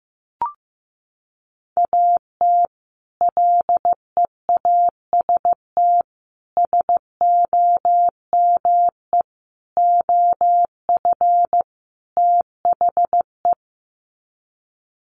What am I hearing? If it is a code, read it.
Morse code, heard as a keyed tone, about 15 wpm, 700 Hz. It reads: AT LEAST SOME OF THE
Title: Sets of 5 Words Encore 15wpm